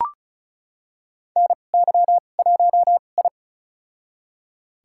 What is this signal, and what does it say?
Morse code, heard as a keyed tone, about 36 wpm, spaced as for 25 wpm, 700 Hz. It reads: NY1I
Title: USA Call Signs - 4 Characters - 2x Character Spacing - 35wpm